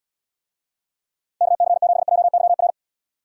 Morse code, heard as a keyed tone, about 72 wpm, 700 Hz. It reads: KC7JYR